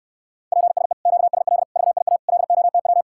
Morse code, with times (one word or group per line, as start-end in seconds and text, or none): 0.5-0.9 PSE
1.1-1.6 QSL
1.8-2.2 VIA
2.3-3.0 LOTW